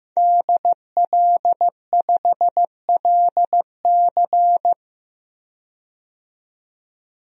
Morse code, heard as a keyed tone, about 15 wpm, 700 Hz. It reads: DL5LC
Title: DXCC-100 Call Signs - 5 Characters - 15wpm